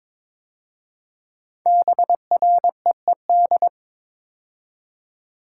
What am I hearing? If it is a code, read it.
Morse code, heard as a keyed tone, about 22 wpm, 700 Hz. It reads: BREED
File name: 5 Letter Words - First 3 Letters Given - 22wpm